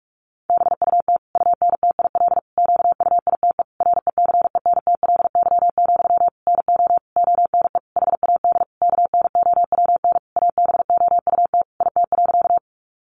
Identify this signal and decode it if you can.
Morse code, heard as a keyed tone, about 44 wpm, 700 Hz. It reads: BUT UNTIL QUITE RECENTLY, NO ONE HAD KNOWN ABOUT IT.
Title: Sentences from Top 1200 Words 45wpm